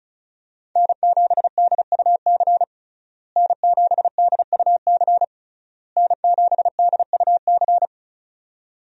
Morse code, heard as a keyed tone, about 36 wpm, 700 Hz. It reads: N7DUC N7DUC N7DUC
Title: USA Call Signs - 5 Characters - Repeated 3x - 35wpm